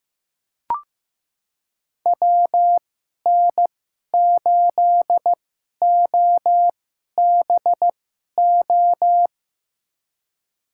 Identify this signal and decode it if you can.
Morse code, heard as a keyed tone, about 16 wpm, spaced as for 11 wpm, 700 Hz. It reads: WN8OBO
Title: USA Call Signs - 6 Characters - 2x Character Spacing - 15wpm